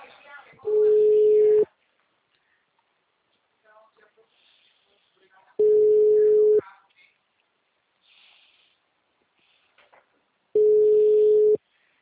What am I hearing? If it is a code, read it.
Ringback tone (Europe, 425 Hz), with speech.